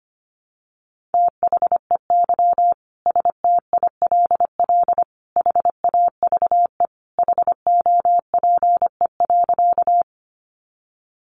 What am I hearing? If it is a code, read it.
Morse code, heard as a keyed tone, about 25 wpm, 700 Hz. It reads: THEY STILL HAVE HOPE.
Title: Sentences from Top 1000 Words 25wpm